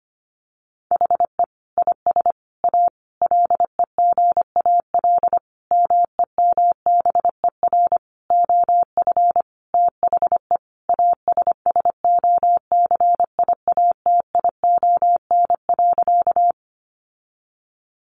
Morse code, heard as a keyed tone, about 25 wpm, 700 Hz. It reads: HE IS A LEGAL MEMBER OF THE ASSOCIATION.
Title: Sentences from Top 1200 Words 25wpm